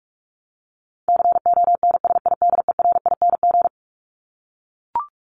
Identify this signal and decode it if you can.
Morse code, heard as a keyed tone, about 44 wpm, 700 Hz. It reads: CONSIDERING
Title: ING Suffix - Top 100 Words with Base Words 45wpm